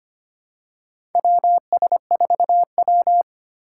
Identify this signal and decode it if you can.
Morse code, heard as a keyed tone, about 25 wpm, 690 Hz. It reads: WS4W
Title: USA Call Signs - 4 Characters - 4x Character Spacing - 25wpm